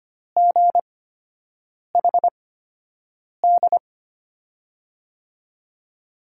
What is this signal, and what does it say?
Morse code, heard as a keyed tone, about 25 wpm, spaced as for 7 wpm, 700 Hz. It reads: GHD
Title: USA Call Signs - 5 Characters - 8x Character Spacing - 25wpm